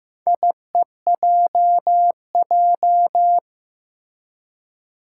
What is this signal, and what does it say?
Morse code, heard as a keyed tone, about 15 wpm, 700 Hz. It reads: IEJJ
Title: Four Challenging Letter-Number Combinations 15wpm